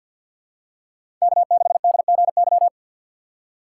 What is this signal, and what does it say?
Morse code, heard as a keyed tone, about 51 wpm, 690 Hz. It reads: K6DGY
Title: USA Call Signs - 2x Character Spacing - 50wpm